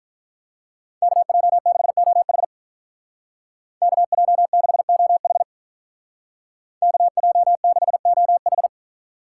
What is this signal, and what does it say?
Morse code, heard as a keyed tone, about 43 wpm, 700 Hz. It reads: KJ6OH KJ6OH KJ6OH
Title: Speed Racing - USA 2x2 Call Signs 35wpm